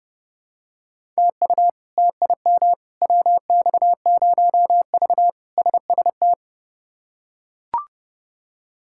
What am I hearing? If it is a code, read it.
Morse code, heard as a keyed tone, about 32 wpm, 700 Hz. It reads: TU TIM WX0V SST
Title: K1USN SST QSO Practice - 30wpm